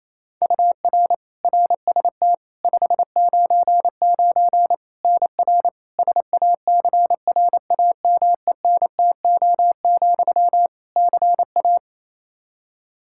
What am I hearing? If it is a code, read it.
Morse code, heard as a keyed tone, about 28 wpm, 700 Hz. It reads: UR RST 599 NR SACRAMENTO, CA
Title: Practice QSO - Exchange 1 - Beginning - 28wpm